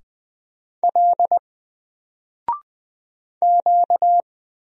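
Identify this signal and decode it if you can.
Morse code, heard as a keyed tone, about 20 wpm, 700 Hz. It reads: L Q
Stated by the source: Single letter and number Morse drill at 20 wpm, from N4LCD.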